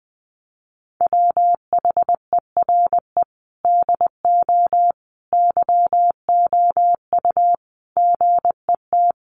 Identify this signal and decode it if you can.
Morse code, heard as a keyed tone, about 19 wpm, 700 Hz. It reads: WHERE DO YOU GET
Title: Sentences from Top 600 Words 20wpm